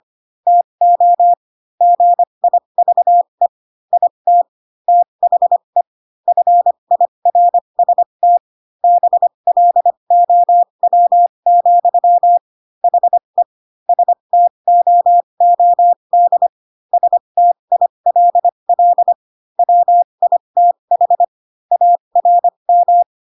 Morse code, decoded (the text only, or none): TO GIVE IT THE FIRST BLOW, HE STOOD STILL WITH ARM